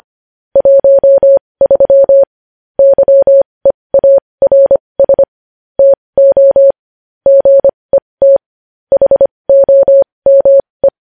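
Morse code, heard as a keyed tone, about 25 wpm, spaced as for 20 wpm, 550 Hz. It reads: 13 YEARS TO GET HOME
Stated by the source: Morse code practice sending news headlines at 20 wpm.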